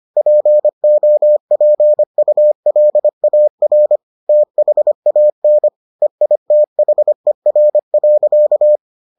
Morse code, heard as a keyed tone, about 25 wpm, 600 Hz. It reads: POPULAR THAN EITHER.